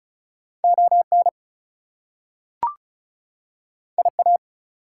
Morse code, heard as a keyed tone, about 36 wpm, 700 Hz. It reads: ON IA
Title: US States and Canadian Provinces 35wpm